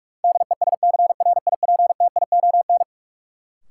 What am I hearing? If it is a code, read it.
Morse code, heard as a keyed tone, about 46 wpm, 700 Hz. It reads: DESCRIPTION
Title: Sentences from Top 1300 Words - New Word Drill - 45wpm